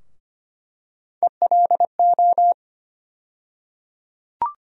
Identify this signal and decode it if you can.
Morse code, heard as a keyed tone, about 25 wpm, 700 Hz. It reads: ELO
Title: USA Call Sign Postfixes - 3 Letters 25wpm